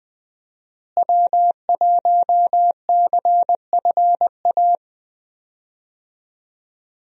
Morse code, heard as a keyed tone, about 20 wpm, 700 Hz. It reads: W1CFA
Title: USA Call Signs - 5 Characters - Repeated 3x - 20wpm